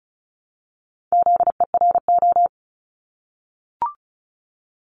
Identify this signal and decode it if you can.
Morse code, heard as a keyed tone, about 36 wpm, 700 Hz. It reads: ZERO
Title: Multi-letter Phonemes Repeated 3x 35wpm